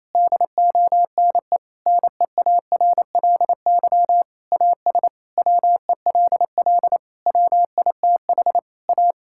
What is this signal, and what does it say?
Morse code, heard as a keyed tone, about 28 wpm, 700 Hz. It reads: DONE NEARLY AS WELL WITH A
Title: Sentences from Top 1600 Words 28wpm